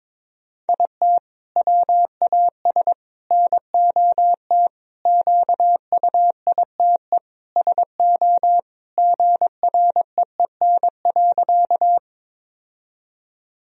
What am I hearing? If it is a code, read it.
Morse code, heard as a keyed tone, about 22 wpm, 700 Hz. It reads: IT WAS NOT QUITE SO GREEN.
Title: Sentences from Top 1000 Words - Review Part 2 - 22wpm